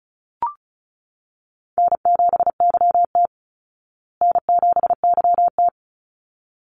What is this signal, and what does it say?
Morse code, heard as a keyed tone, about 34 wpm, 700 Hz. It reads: N7YT N7YT
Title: Easy Call Signs Repeated 2x - 35wpm